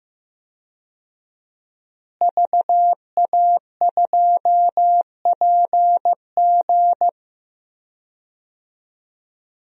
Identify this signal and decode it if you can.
Morse code, heard as a keyed tone, about 15 wpm, 700 Hz. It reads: VA2PG
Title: DXCC-100 Call Signs - 5 Characters - Repeated 3x - 15wpm